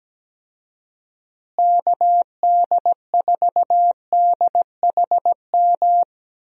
Morse code, heard as a keyed tone, about 17 wpm, 700 Hz. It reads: KD4DHM
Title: USA Call Signs - 6 Characters - Repeated 2x - 17wpm